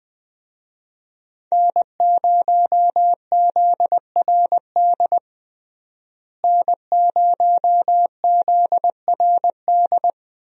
Morse code, heard as a keyed tone, about 21 wpm, 700 Hz. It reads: N0ZRD N0ZRD
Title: USA Call Signs - 5 Characters - Repeated 2x - 20wpm